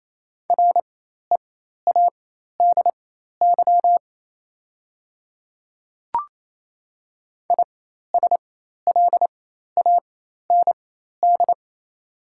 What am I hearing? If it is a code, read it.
Morse code, heard as a keyed tone, about 28 wpm, spaced as for 13 wpm, 690 Hz. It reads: READY ISLAND
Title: 4x Character Spacing - Top 500 Words - 28wpm